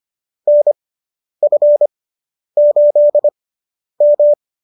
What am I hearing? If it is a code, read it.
Morse code, heard as a keyed tone, about 25 wpm, spaced as for 10 wpm, 590 Hz. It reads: NF8M